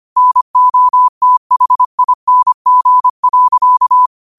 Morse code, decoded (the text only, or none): NOTHING.